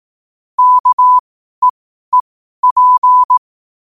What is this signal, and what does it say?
Morse code, heard as a keyed tone, about 18 wpm, spaced as for 12 wpm, 1000 Hz. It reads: KEEP
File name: qotdfiles / 12wpm